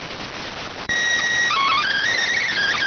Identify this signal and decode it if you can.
SSTV (slow-scan television), PD 290.